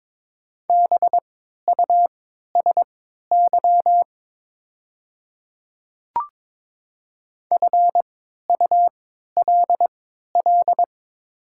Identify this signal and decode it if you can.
Morse code, heard as a keyed tone, about 22 wpm, spaced as for 12 wpm, 700 Hz. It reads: BUSY FULL
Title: 3x Character Spacing - Top 500 Words - 22wpm